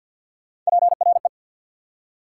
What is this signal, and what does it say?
Morse code, heard as a keyed tone, about 69 wpm, spaced as for 59 wpm, 690 Hz. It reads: PRE
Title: Speed-Racing Prefixes and Suffixes 50wpm